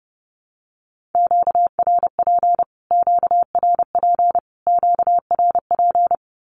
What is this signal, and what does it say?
Morse code, heard as a keyed tone, about 30 wpm, 700 Hz. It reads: QRP QRP QRP
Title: Q-Codes Repeated 3x 30wpm